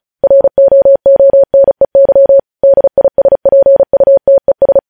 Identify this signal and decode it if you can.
Morse code, heard as a keyed tone, about 36 wpm, 550 Hz. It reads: ROONEY DISPUTES